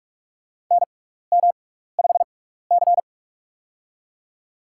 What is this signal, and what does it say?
Morse code, heard as a keyed tone, about 46 wpm, spaced as for 16 wpm, 700 Hz. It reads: NM5C
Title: USA Call Signs - 4 Characters - 6x Character Spacing - 45wpm